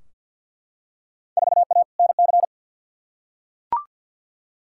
Morse code, heard as a keyed tone, about 51 wpm, 700 Hz. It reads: 4A NC
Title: Field Day Exchanges - 50wpm